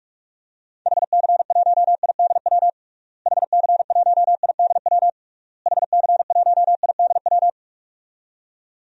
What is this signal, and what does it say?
Morse code, heard as a keyed tone, about 46 wpm, 700 Hz. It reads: HC1IDW HC1IDW HC1IDW
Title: DXCC-100 Call Signs - 6 Characters - Repeated 3x - 45wpm